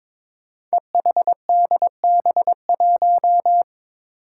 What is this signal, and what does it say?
Morse code, heard as a keyed tone, about 22 wpm, 700 Hz. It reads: EHDB1
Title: Five Challenging Letter-Number Combinations 22wpm